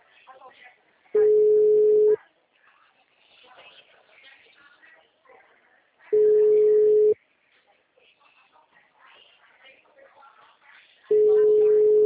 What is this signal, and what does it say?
Ringback tone (Europe, 425 Hz), with speech.